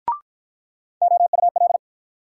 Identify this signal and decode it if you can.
Morse code, heard as a keyed tone, about 54 wpm, 700 Hz. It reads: OUL